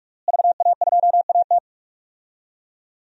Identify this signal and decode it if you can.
Morse code, heard as a keyed tone, about 43 wpm, 700 Hz. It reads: VA2AT
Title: DXCC-100 Call Signs - 5 Characters - Repeated 2x - 45wpm